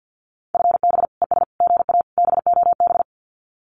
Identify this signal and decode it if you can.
Morse code, heard as a keyed tone, about 50 wpm, 700 Hz. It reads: FB ES GA BOB
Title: Common QSO Phrases Encore 50wpm